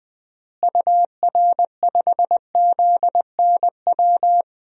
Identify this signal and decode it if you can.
Morse code, heard as a keyed tone, about 21 wpm, 700 Hz. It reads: UR5ZNW
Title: DXCC-100 Call Signs - 6 Characters - Repeated 3x - 20wpm